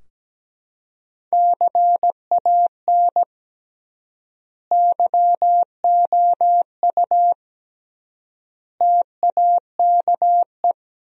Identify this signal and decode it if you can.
Morse code, heard as a keyed tone, about 17 wpm, 700 Hz. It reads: CAN YOU TAKE
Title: 3x Word-Spacing - Sentences from Top 200 Words - 17wpm